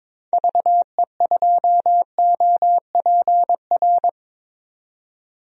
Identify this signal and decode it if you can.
Morse code, heard as a keyed tone, about 22 wpm, 700 Hz. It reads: VE2OPR
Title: DXCC-100 Call Signs - 6 Characters - 22wpm